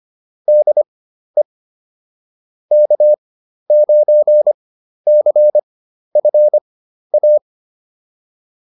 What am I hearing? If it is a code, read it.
Morse code, heard as a keyed tone, about 25 wpm, spaced as for 12 wpm, 600 Hz. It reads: DE K9CFA
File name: QSO 202 12